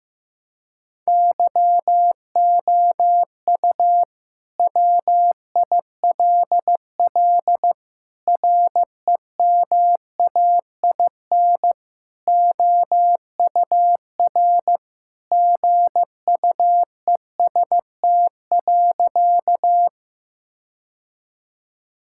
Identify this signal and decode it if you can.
Morse code, heard as a keyed tone, about 16 wpm, 700 Hz. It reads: YOU WILL REMAIN OUR GUEST.
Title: Sentences from Top 1600 Words 15wpm